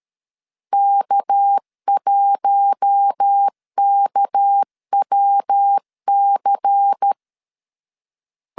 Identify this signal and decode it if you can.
Morse code, heard as a keyed tone, about 13 wpm, 800 Hz. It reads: K1KWC
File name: LEZIONI / 6 PKX0 / 6 PK 12wpm 2006